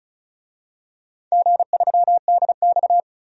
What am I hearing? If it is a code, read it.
Morse code, heard as a keyed tone, about 36 wpm, 690 Hz. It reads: G3DX
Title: DXCC-100 Call Signs - 4 Characters - 35wpm